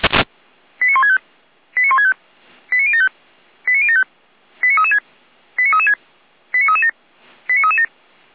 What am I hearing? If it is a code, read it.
Five-tone selective call (ZVEI), heard as quick runs of five tones, tones of about 80 ms. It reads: 89165 89165 89075 89075 89336 89336 89337 89337